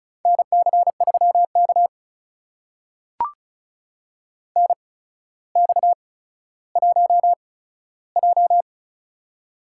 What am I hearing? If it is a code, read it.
Morse code, heard as a keyed tone, about 36 wpm, spaced as for 15 wpm, 700 Hz. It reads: NC3K NX1J
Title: USA Call Signs - 4 Characters - 8x Character Spacing - 35wpm